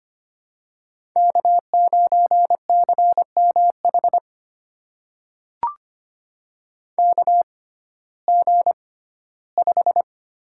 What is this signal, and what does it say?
Morse code, heard as a keyed tone, about 25 wpm, spaced as for 15 wpm, 700 Hz. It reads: K9CMH KG5
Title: USA Call Signs - 5 Characters - 6x Character Spacing - 25wpm